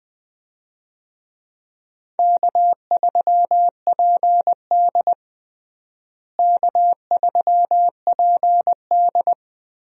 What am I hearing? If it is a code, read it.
Morse code, heard as a keyed tone, about 20 wpm, 700 Hz. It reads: K3PD K3PD
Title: USA Call Signs - 4 Characters - Repeated 3x - 20wpm